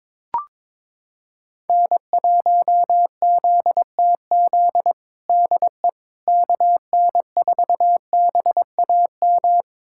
Morse code, heard as a keyed tone, about 22 wpm, 700 Hz. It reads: N1ZTZ DE KN4BAM
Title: USA Callsign DE Callsign 22wpm